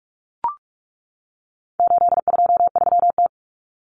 Morse code, heard as a keyed tone, about 44 wpm, 700 Hz. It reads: 823T
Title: Common Cut Numbers - 4-Digit 45wpm